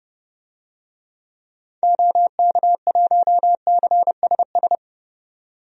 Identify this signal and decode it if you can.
Morse code, heard as a keyed tone, about 30 wpm, 700 Hz. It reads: OK1CSS
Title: DXCC-100 Call Signs - 6 Characters - 30wpm